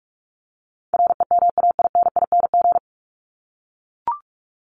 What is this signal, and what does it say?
Morse code, heard as a keyed tone, about 44 wpm, 700 Hz. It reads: REMAINING
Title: ING Suffix - Top 100 Words with Base Words 45wpm